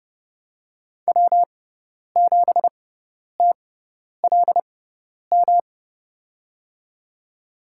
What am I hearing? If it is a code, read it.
Morse code, heard as a keyed tone, about 30 wpm, spaced as for 10 wpm, 700 Hz. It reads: W7TLM
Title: USA Call Signs - 5 Characters - 6x Character Spacing - 30wpm